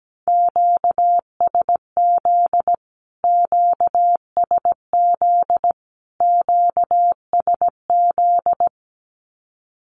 Morse code, heard as a keyed tone, about 17 wpm, 700 Hz. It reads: QSZ QSZ QSZ
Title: Q-Codes Repeated 3x 17wpm